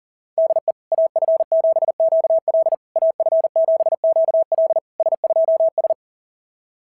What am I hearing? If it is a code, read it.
Morse code, heard as a keyed tone, about 40 wpm, 690 Hz. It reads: DE AF7QL AF7QL S2S
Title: SOTA QSO Practice 40wpm